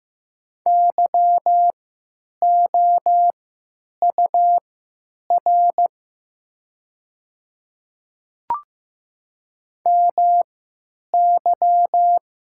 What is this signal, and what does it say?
Morse code, heard as a keyed tone, about 15 wpm, spaced as for 9 wpm, 700 Hz. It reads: YOUR MY
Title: 3x Character Spacing - Top 100 Words - 15wpm